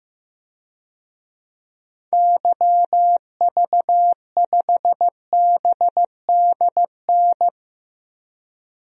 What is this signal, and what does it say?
Morse code, heard as a keyed tone, about 15 wpm, 700 Hz. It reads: YV5BDN
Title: DXCC-100 Call Signs - 6 Characters - Repeated 2x - 15wpm